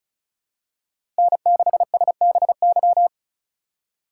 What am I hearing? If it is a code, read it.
Morse code, heard as a keyed tone, about 36 wpm, 700 Hz. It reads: N6SBY
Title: USA Call Signs - 5 Characters - 3x Character Spacing - 35wpm